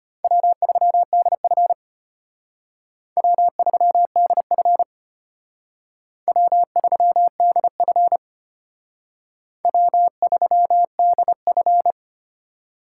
Morse code, heard as a keyed tone, about 34 wpm, 700 Hz. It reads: W3DF W3DF W3DF W3DF